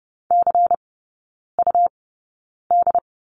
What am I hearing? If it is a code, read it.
Morse code, heard as a keyed tone, about 30 wpm, spaced as for 9 wpm, 700 Hz. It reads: CUD
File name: Lesson 17e - Letter u - Three Characters 30wpm